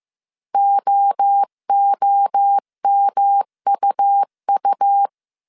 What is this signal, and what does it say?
Morse code, heard as a keyed tone, about 15 wpm, 800 Hz. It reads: OOMUU